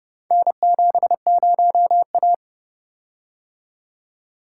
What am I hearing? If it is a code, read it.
Morse code, heard as a keyed tone, about 29 wpm, 700 Hz. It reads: N70A